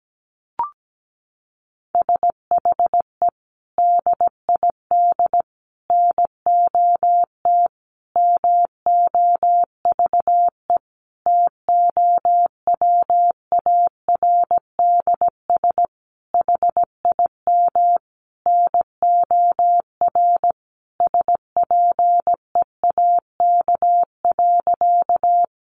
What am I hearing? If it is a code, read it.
Morse code, heard as a keyed tone, about 17 wpm, 700 Hz. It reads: SHE DID NOT MOVE TOWARDS HIM NOR SPEAK.
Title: Sentences from Top 1500 Words 17wpm